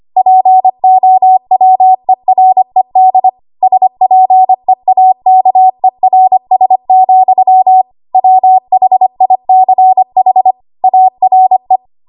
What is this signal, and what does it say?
Morse code, heard as a keyed tone, about 25 wpm, 750 Hz. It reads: POWERED SPEAKERS, WHICH ARE
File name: Archive / 25 WPM / 25WPM